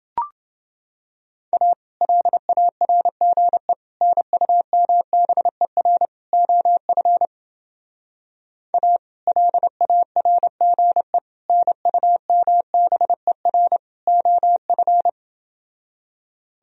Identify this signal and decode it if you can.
Morse code, heard as a keyed tone, about 29 wpm, 700 Hz. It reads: A LARGE NUMBER OF A LARGE NUMBER OF